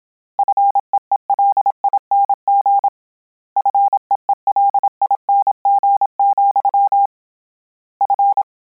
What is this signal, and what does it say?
Morse code, heard as a keyed tone, about 27 wpm, 800 Hz. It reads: FEELING FEELING, F
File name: Combinations / Combinations-mp3s / Combinations@25